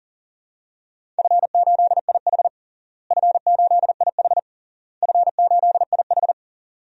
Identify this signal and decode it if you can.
Morse code, heard as a keyed tone, about 40 wpm, 700 Hz. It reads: F8IH F8IH F8IH
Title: DXCC-100 Call Signs - 4 Characters - Repeated 3x - 40wpm